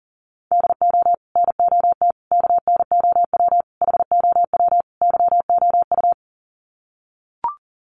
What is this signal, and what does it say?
Morse code, heard as a keyed tone, about 40 wpm, 700 Hz. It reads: DO NOT KNOW HOW YOU
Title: Sets of 5 Words Encore 40wpm